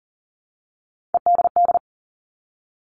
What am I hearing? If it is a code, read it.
Morse code, heard as a keyed tone, about 40 wpm, 700 Hz. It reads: EDD
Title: All Cut Numbers - 3-Digit 40wpm